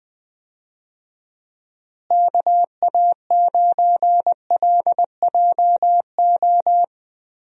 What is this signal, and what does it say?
Morse code, heard as a keyed tone, about 21 wpm, 700 Hz. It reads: KA9LJO
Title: USA Call Signs - 6 Characters - Repeated 2x - 20wpm